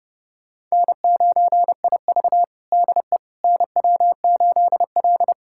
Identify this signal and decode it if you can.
Morse code, heard as a keyed tone, about 30 wpm, 700 Hz. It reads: N9IV DE NW8L
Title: USA Callsign DE Callsign - 4 Characters - 30wpm